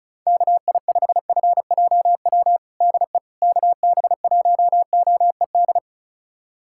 Morse code, heard as a keyed tone, about 36 wpm, 700 Hz. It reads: KI5FJW DE KB1OED
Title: USA Callsign DE Callsign - 6 Characters - 35wpm